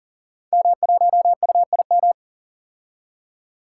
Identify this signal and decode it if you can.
Morse code, heard as a keyed tone, about 38 wpm, 700 Hz. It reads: M1UIM
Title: Code Groups - 5 Letter:Numbers - 40wpm